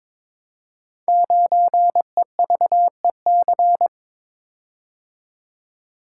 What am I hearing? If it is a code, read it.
Morse code, heard as a keyed tone, about 23 wpm, 700 Hz. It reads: 9EVEC
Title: Code Groups - 5 Letter:Numbers - 22wpm